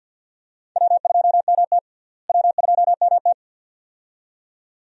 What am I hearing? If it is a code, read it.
Morse code, heard as a keyed tone, about 46 wpm, 700 Hz. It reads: W2MT W2MT
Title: USA Call Signs - 4 Characters - Repeated 2x - 50wpm